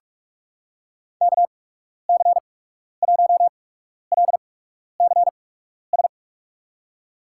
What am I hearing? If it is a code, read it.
Morse code, heard as a keyed tone, about 42 wpm, spaced as for 12 wpm, 700 Hz. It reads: KC1LCS
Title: USA Call Signs - 6 Characters - 8x Character Spacing - 45wpm